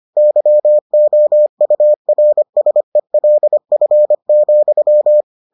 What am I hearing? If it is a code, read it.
Morse code, heard as a keyed tone, about 25 wpm, 600 Hz. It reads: YOURSELF,